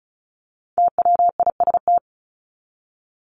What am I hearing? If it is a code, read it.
Morse code, heard as a keyed tone, about 34 wpm, 700 Hz. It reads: TWIST
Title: Top 1751-2000 Words 35wpm